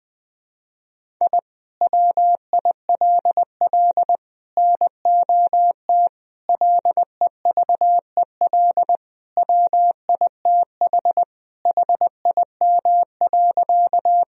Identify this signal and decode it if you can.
Morse code, heard as a keyed tone, about 20 wpm, 700 Hz. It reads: I WILL NOT LEVEL WITH HIM.
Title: Sentences from Top 1000 Words - Review Part 4 - 20wpm